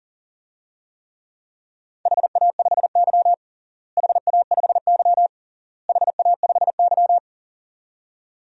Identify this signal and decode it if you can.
Morse code, heard as a keyed tone, about 37 wpm, 700 Hz. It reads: HA5Y HA5Y HA5Y
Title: DXCC-100 Call Signs - 4 Characters - Repeated 3x - 40wpm